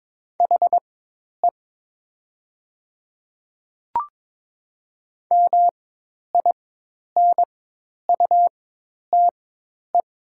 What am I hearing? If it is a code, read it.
Morse code, heard as a keyed tone, about 22 wpm, spaced as for 10 wpm, 690 Hz. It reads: HE MINUTE
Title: 4x Character Spacing - Top 500 Words - 22wpm